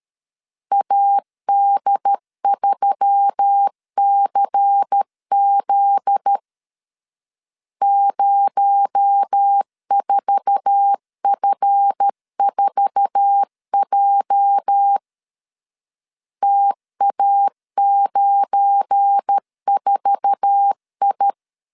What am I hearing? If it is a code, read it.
Morse code, heard as a keyed tone, about 13 wpm, 800 Hz. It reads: AD3CZ 04F4J TA94I
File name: LEZIONI / 8 YJ45 / 8 YJ4 12wpm 2151